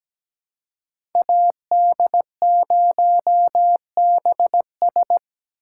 Morse code, heard as a keyed tone, about 20 wpm, 700 Hz. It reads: AD0BS
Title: Speed Racing - USA Call Signs 17wpm